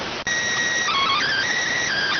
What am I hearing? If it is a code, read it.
SSTV (slow-scan television), PD 290.